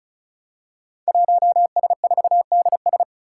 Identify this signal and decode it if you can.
Morse code, heard as a keyed tone, about 35 wpm, 700 Hz. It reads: 1S4DS
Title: Five Challenging Letter-Number Combinations 35wpm